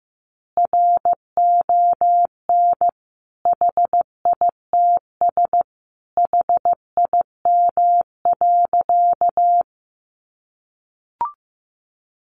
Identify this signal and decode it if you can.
Morse code, heard as a keyed tone, about 15 wpm, 700 Hz. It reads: RON HITS HIM.
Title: Additional Easy Phrases 15wpm